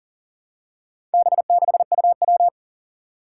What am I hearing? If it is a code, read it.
Morse code, heard as a keyed tone, about 38 wpm, 700 Hz. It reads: B6UW
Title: Four Challenging Letter-Number Combinations 40wpm